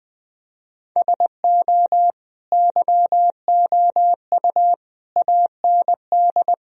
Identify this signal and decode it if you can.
Morse code, heard as a keyed tone, about 20 wpm, 700 Hz. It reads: SO YOU AND
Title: Sentences from Top 1000 Words - Review Part 4 - 20wpm